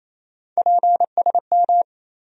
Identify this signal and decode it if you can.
Morse code, heard as a keyed tone, about 28 wpm, 700 Hz. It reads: PSM
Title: USA Call Sign Postfixes - 3 Letters 28wpm